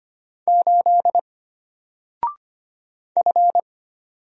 Morse code, heard as a keyed tone, about 25 wpm, 700 Hz. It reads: 8 F
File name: Single-Letter-Number-25wpm